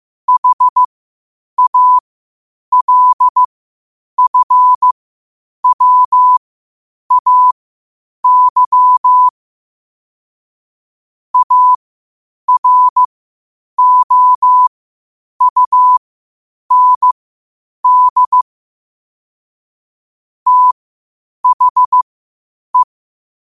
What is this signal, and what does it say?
Morse code, heard as a keyed tone, about 15 wpm, spaced as for 8 wpm, 1000 Hz. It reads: HALFWAY AROUND THE